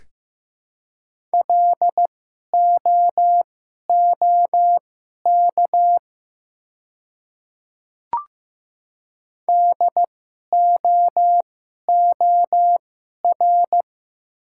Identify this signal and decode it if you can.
Morse code, heard as a keyed tone, about 15 wpm, spaced as for 11 wpm, 700 Hz. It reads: LOOK DOOR
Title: 2x Character Spacing - Top 500 Words - 15wpm